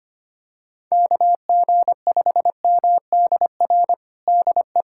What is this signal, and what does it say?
Morse code, heard as a keyed tone, about 25 wpm, 700 Hz. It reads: KG5MDR DE
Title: USA Callsign DE Callsign 25wpm